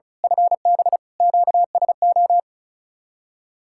Morse code, heard as a keyed tone, about 38 wpm, 700 Hz. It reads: FB QSO
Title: Speed-Racing Common QSO Phrases - 35wpm